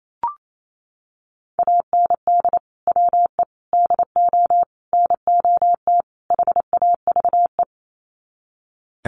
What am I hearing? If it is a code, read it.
Morse code, heard as a keyed tone, about 28 wpm, 700 Hz. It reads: AND WE DO NOT HAVE
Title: Sets of 5 Words Encore 28wpm